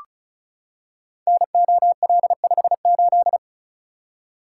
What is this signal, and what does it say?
Morse code, heard as a keyed tone, about 36 wpm, 700 Hz. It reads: NOL58
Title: Code Groups - 5 Letter:Numbers - 35wpm